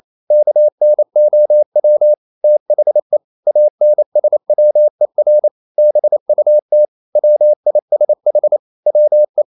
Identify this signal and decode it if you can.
Morse code, heard as a keyed tone, about 28 wpm, 600 Hz. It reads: KNOW THE ANSWER BUT WISH WE
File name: CW / Practice / EnglishQuotations0228